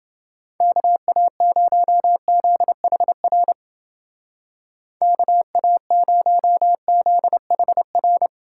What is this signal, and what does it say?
Morse code, heard as a keyed tone, about 29 wpm, 700 Hz. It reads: KA0ZHR KA0ZHR